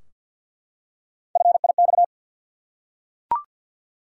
Morse code, heard as a keyed tone, about 50 wpm, 700 Hz. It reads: FIX
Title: Three Letter Words - 50wpm